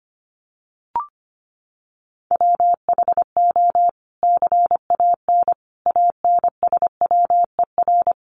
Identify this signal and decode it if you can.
Morse code, heard as a keyed tone, about 25 wpm, 700 Hz. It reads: WHO CAN ANSWER
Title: Sentences from Top 300 Words 25wpm